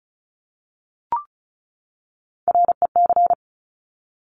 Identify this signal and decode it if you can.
Morse code, heard as a keyed tone, about 36 wpm, 690 Hz. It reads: REC
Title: Top Three Letter Combinations in Words 35wpm